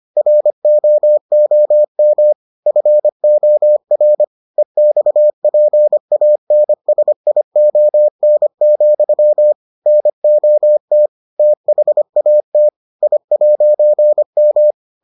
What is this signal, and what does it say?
Morse code, heard as a keyed tone, about 25 wpm, 600 Hz. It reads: ROOM FOR EXPANSION, NOT THAT I'M